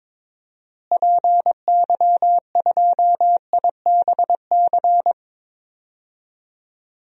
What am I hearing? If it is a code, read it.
Morse code, heard as a keyed tone, about 22 wpm, 700 Hz. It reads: PY2IBC
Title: DXCC-100 Call Signs - 6 Characters - Repeated 3x - 22wpm